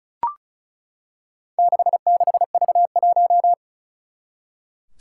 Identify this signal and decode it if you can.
Morse code, heard as a keyed tone, about 36 wpm, 690 Hz. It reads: 66V1
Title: Four Challenging Letter-Number Combinations 35wpm